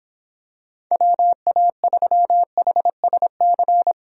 Morse code, heard as a keyed tone, about 26 wpm, 700 Hz. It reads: WA3HSC